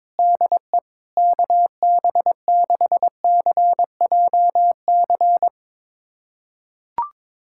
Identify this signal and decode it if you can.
Morse code, heard as a keyed tone, about 22 wpm, 700 Hz. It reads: DE KB6CJC
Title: USA Callsign DE Callsign - 6 Characters - 22wpm